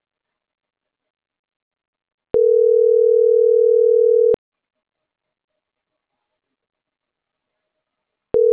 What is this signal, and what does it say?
Ringback tone (North America), 440 + 480 Hz.